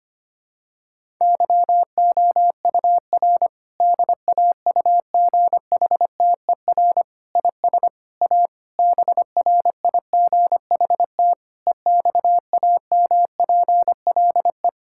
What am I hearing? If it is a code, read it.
Morse code, heard as a keyed tone, about 25 wpm, 700 Hz. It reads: YOUR DAUGHTER IS A BRIGHT EXAMPLE
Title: Sentences from Top 1600 Words 25wpm